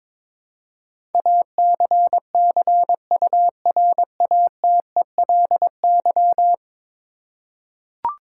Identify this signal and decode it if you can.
Morse code, heard as a keyed tone, about 22 wpm, 700 Hz. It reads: ACCURATELY
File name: practice / 22wpm / 10 Letter Words - First 3 Letters Given - 22wpm